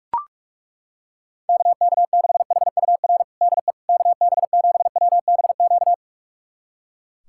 Morse code, heard as a keyed tone, about 42 wpm, 700 Hz. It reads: KK6HUR DE KB7WBQ
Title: USA Callsign DE Callsign - 6 Characters - 45wpm